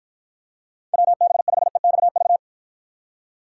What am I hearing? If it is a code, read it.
Morse code, heard as a keyed tone, about 53 wpm, 700 Hz. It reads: WB5EXV